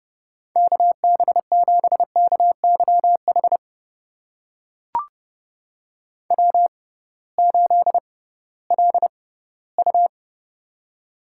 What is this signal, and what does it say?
Morse code, heard as a keyed tone, about 30 wpm, spaced as for 18 wpm, 700 Hz. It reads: KB7KYH W8LU
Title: USA Call Signs - 6x Character Spacing - 30wpm